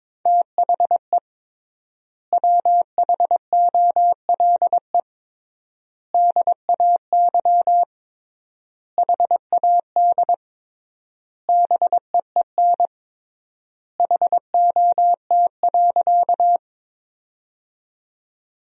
Morse code, heard as a keyed tone, about 21 wpm, 700 Hz. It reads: THE WHOLE DAY HAD BEEN HOT.
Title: 3x Word-Spacing - Sentences from Top 500 Words - 22wpm